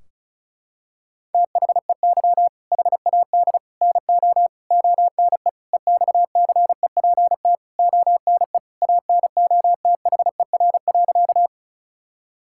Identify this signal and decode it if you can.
Morse code, heard as a keyed tone, about 36 wpm, 700 Hz. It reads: THEY HAD NO ONE EXCEPT ONE ANOTHER.
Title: Sentences from Top 1600 Words 35wpm